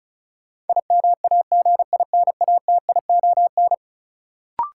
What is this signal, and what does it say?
Morse code, heard as a keyed tone, about 36 wpm, 700 Hz. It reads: IMAGINATION
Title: Sentences from Top 1300 Words - New Word Drill - 35wpm